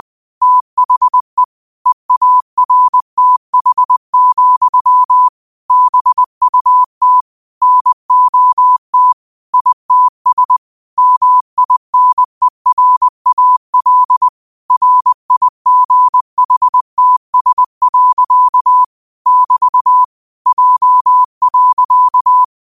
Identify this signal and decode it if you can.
Morse code, heard as a keyed tone, about 20 wpm, 1000 Hz. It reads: THE EARTH, BUT NOT ITS MINERAL RIGHTS. <BT> J.
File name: qotdfiles / 20wpm